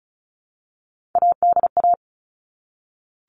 Morse code, heard as a keyed tone, about 34 wpm, 700 Hz. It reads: ADA
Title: All Cut Numbers - 3-Digit 35wpm